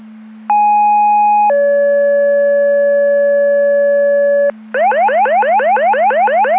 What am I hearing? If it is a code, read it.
Two-tone page, heard as a one-second tone, then a three-second tone, about 830 then 570 Hz, with noise. It is heard from 0.5 to 4.5 s.